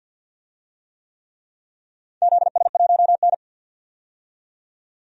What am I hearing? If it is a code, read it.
Morse code, heard as a keyed tone, about 52 wpm, 700 Hz. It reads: ZS1N
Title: DXCC-100 Call Signs - 4 Characters - 50wpm